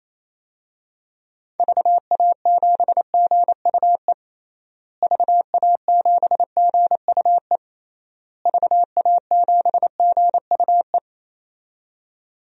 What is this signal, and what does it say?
Morse code, heard as a keyed tone, about 27 wpm, 700 Hz. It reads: VA7GUE VA7GUE VA7GUE
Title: DXCC-100 Call Signs - 6 Characters - Repeated 3x - 28wpm